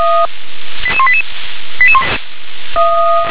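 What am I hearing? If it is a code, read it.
Five-tone selective call (ZVEI), heard as quick runs of five tones, tones of about 70 ms. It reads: 88188
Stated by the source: ZVEI five-tone call received by SDR.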